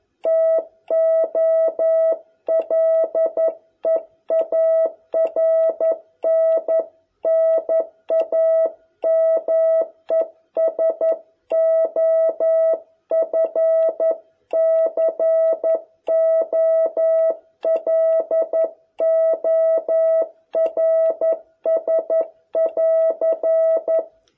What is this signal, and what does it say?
Morse code, heard as a keyed tone, about 10 wpm, 650 Hz. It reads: TO LEARN NAMES OF COLORS <AR>